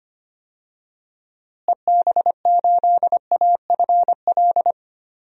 Morse code, heard as a keyed tone, about 25 wpm, 700 Hz. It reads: EB8AFL
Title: DXCC-100 Call Signs - 6 Characters - Repeated 2x - 25wpm